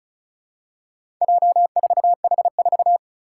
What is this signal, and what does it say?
Morse code, heard as a keyed tone, about 35 wpm, 700 Hz. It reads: J4H4
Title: Four Challenging Letter-Number Combinations 35wpm